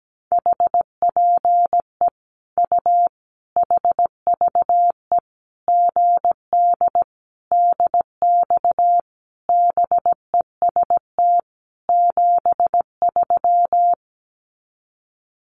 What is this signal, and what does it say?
Morse code, heard as a keyed tone, about 17 wpm, 700 Hz. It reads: HPE U HVE GD DX BEST 73
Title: Common QSO Phrases 17wpm